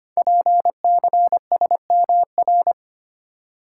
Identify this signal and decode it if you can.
Morse code, heard as a keyed tone, about 25 wpm, 700 Hz. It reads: PCSMR
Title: Code Groups - 5 Letter:Numbers - 25wpm